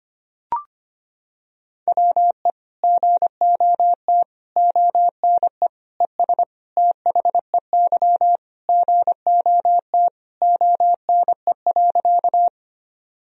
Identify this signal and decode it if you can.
Morse code, heard as a keyed tone, about 25 wpm, 700 Hz. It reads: WE GOT ONE ES THEY GOT ONE.
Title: Sentences from Top 300 Words and Abbreviations 25wpm